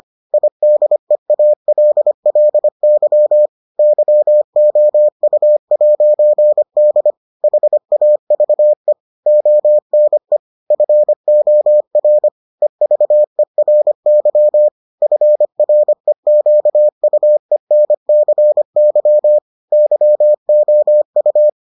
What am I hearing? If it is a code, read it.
Morse code, heard as a keyed tone, about 25 wpm, 600 Hz. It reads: IDEALLY YOU'D HAVE ONE FOR EVERY FREQUENCY YOU